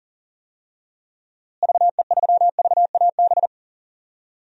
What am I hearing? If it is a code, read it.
Morse code, heard as a keyed tone, about 40 wpm, 700 Hz. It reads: VE3VAB
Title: DXCC-100 Call Signs - 6 Characters - Repeated 3x - 40wpm